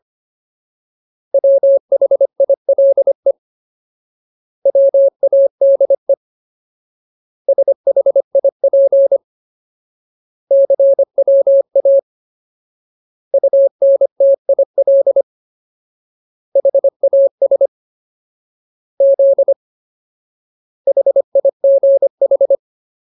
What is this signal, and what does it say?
Morse code, heard as a keyed tone, about 25 wpm, 550 Hz. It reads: WHILE WADE SHIP CWA UNTIL HAS Z HIGH
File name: Ultimate Word List / 25WPM Ultimate